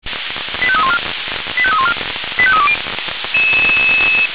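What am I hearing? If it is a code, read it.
Five-tone selective call (ZVEI), heard as quick runs of five tones, tones of about 70 ms. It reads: 85314 85314 85320
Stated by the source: ZVEI five-tone selective call 85314 received on an SDR.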